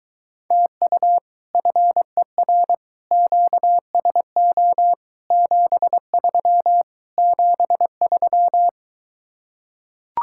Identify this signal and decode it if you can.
Morse code, heard as a keyed tone, about 22 wpm, 700 Hz. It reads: TU FER QSO 73 73